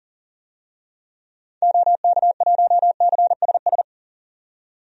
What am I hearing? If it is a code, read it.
Morse code, heard as a keyed tone, about 40 wpm, 700 Hz. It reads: OK1CSS
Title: DXCC-100 Call Signs - 6 Characters - 40wpm